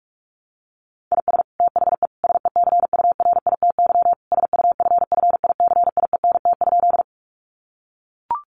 Morse code, heard as a keyed tone, about 44 wpm, 700 Hz. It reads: IS THE SECURITY SUFFICIENT?
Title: Sentences from Top 1400 Words 45wpm